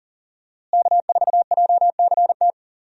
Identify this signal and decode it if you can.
Morse code, heard as a keyed tone, about 40 wpm, 700 Hz. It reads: K4JCT
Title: USA Call Signs - 5 Characters - Repeated 3x - 40wpm